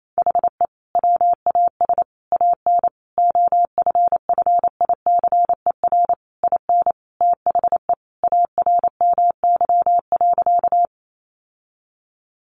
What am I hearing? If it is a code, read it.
Morse code, heard as a keyed tone, about 28 wpm, 700 Hz. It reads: HE WAS AN OFFICER IN THE ARMY.
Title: Sentences from Top 1200 Words 28wpm